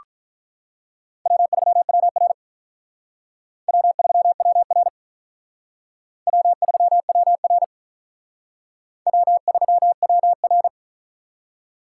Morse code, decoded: W3WR W3WR W3WR W3WR

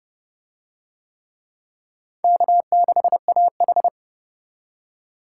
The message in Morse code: K6AH

